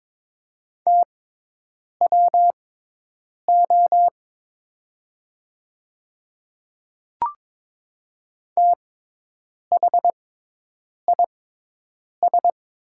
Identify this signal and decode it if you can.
Morse code, heard as a keyed tone, about 23 wpm, spaced as for 8 wpm, 700 Hz. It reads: TWO THIS
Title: 6x Character Spacing - Top 100 Words - 22wpm